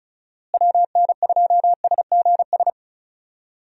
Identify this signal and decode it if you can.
Morse code, heard as a keyed tone, about 34 wpm, 700 Hz. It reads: WN2SGS
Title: USA Call Signs - 6 Characters - Repeated 3x - 35wpm